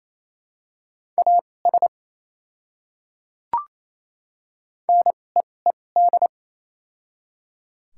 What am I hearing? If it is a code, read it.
Morse code, heard as a keyed tone, about 27 wpm, spaced as for 20 wpm, 700 Hz. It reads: AS NEED
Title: 2x Character Spacing - Top 200 Words - 28wpm